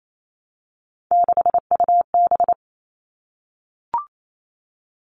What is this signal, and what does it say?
Morse code, heard as a keyed tone, about 28 wpm, 700 Hz. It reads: 6UB
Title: All Cut Numbers - 3-Digit 28wpm